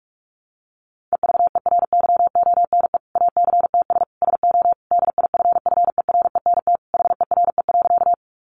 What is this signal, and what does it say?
Morse code, heard as a keyed tone, about 44 wpm, 700 Hz. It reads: EVERYONE ACTS SO DIFFERENT HERE.
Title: Easy Phrases 45wpm